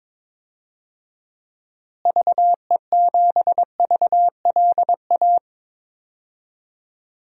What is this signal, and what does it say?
Morse code, heard as a keyed tone, about 22 wpm, 700 Hz. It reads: VE7VLA
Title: DXCC-100 Call Signs - 6 Characters - Repeated 2x - 22wpm